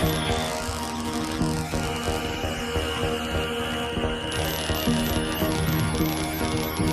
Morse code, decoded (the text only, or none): none